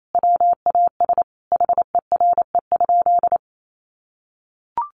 Morse code, heard as a keyed tone, about 28 wpm, 700 Hz. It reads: WAS HERE?
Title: Sentences from Top 1500 Words 28wpm